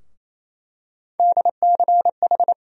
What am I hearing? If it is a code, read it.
Morse code, heard as a keyed tone, about 28 wpm, 700 Hz. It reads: DCH
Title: USA Call Sign Postfixes - 3 Letters 28wpm